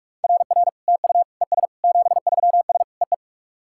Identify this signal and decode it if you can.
Morse code, heard as a keyed tone, about 48 wpm, 690 Hz. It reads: RR TU ES 73S EE
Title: SOTA QSO Practice 45wpm